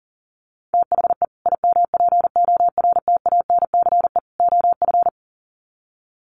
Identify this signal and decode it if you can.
Morse code, heard as a keyed tone, about 40 wpm, 700 Hz. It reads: THE IMPORTANCE OF
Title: Sets of 3 Words 40wpm